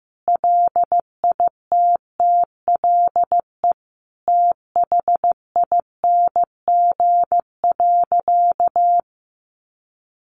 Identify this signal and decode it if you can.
Morse code, heard as a keyed tone, about 15 wpm, 700 Hz. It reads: LITTLE THING.
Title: Sentences from Top 1000 Words 15wpm